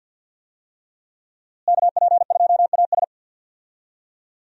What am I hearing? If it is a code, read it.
Morse code, heard as a keyed tone, about 46 wpm, 700 Hz. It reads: KP2AS
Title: USA Call Signs - 5 Characters - Repeated 3x - 50wpm